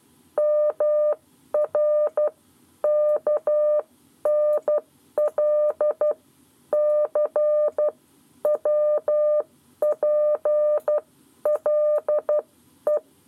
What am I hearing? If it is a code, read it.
Morse code, heard as a keyed tone, about 12 wpm, spaced as for 10 wpm, 600 Hz. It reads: MR KNL C WPLE
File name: 7 Lezione cw 1